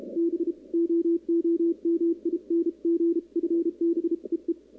Morse code, heard as a keyed tone, about 33 wpm, 340 Hz, with radio noise. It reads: BOOMING FB EE
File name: contests / 2019-cw-wpx-kt5j-k5pi- -2330 0-11